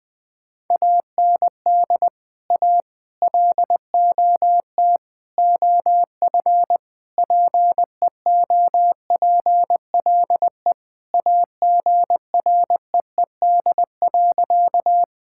Morse code, heard as a keyed tone, about 20 wpm, 700 Hz. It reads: AND A LOT OF PEOPLE AGREED.
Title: Sentences with 5-Word Phrases - 20wpm